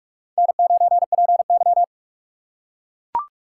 Morse code, heard as a keyed tone, about 41 wpm, 700 Hz. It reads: N9PY